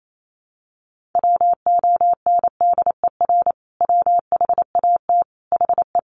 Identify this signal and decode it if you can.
Morse code, heard as a keyed tone, about 28 wpm, 700 Hz. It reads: WONDER WHAT HE
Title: Sentences from Top 500 Words 28wpm